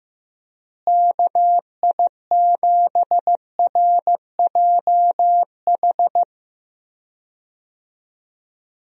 Morse code, heard as a keyed tone, about 15 wpm, 700 Hz. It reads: KI7RJH
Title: USA Call Signs - 6 Characters - Repeated 3x - 15wpm